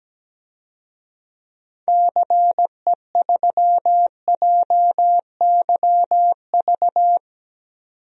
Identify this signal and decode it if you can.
Morse code, heard as a keyed tone, about 17 wpm, 700 Hz. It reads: CE3JYV
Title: DXCC-100 Call Signs - 6 Characters - 17wpm